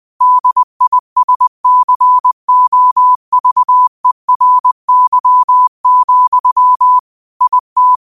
Morse code, heard as a keyed tone, about 20 wpm, 1000 Hz. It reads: DISCOVERY, IT
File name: qotdfiles / 20wpm